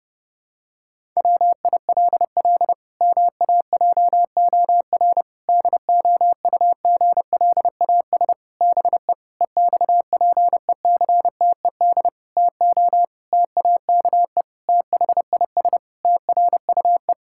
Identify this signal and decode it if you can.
Morse code, heard as a keyed tone, about 32 wpm, 700 Hz. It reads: WILL MAJOR DOUGLAS BE EXPECTED TO TAKE THIS TRUE
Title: Pangram Challenge v2 - 30wpm